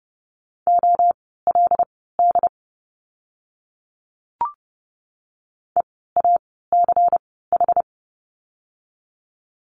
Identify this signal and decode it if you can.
Morse code, heard as a keyed tone, about 30 wpm, spaced as for 17 wpm, 700 Hz. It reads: OLD EACH
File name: course / 30:10wpm / 230 Lesson 37f - Letter x - Words 30wpm